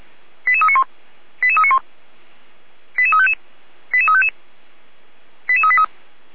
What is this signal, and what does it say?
Five-tone selective call (ZVEI), heard as quick runs of five tones, tones of about 70 ms. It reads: 88391 88391 88355 88355 88383